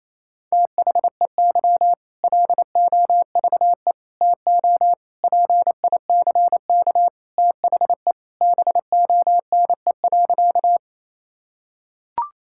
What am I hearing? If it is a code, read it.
Morse code, heard as a keyed tone, about 27 wpm, 700 Hz. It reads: THEY LOVE TO PICK THE BONE.
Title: Sentences from Top 1600 Words 28wpm